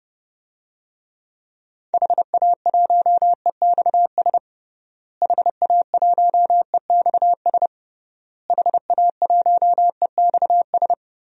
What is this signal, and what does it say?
Morse code, heard as a keyed tone, about 30 wpm, 700 Hz. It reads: HA1EXS HA1EXS HA1EXS
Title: DXCC-100 Call Signs - 6 Characters - Repeated 3x - 30wpm